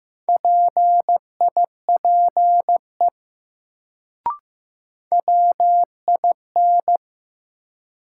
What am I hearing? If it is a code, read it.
Morse code, heard as a keyed tone, about 15 wpm, 700 Hz. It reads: PIPE WIN
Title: Sentences from Top 1500 Words - New Word Drill Alt - 15wpm